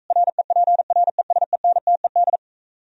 Morse code, heard as a keyed tone, about 46 wpm, 700 Hz. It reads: REPRESENTED